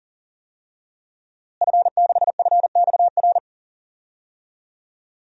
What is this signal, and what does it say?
Morse code, heard as a keyed tone, about 40 wpm, 700 Hz. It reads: F6FXR
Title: DXCC-100 Call Signs - 5 Characters - Repeated 2x - 40wpm